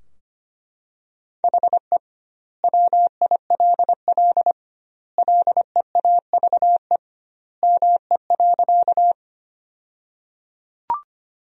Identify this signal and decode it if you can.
Morse code, heard as a keyed tone, about 25 wpm, 700 Hz. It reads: HE WILL LEAVE ME.
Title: 2x Word-Spacing - Sentences from Top 400 Words - 25wpm